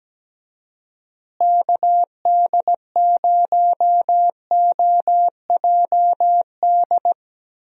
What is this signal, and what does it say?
Morse code, heard as a keyed tone, about 17 wpm, 700 Hz. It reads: KD0OJD
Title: USA Call Signs - 6 Characters - Repeated 2x - 17wpm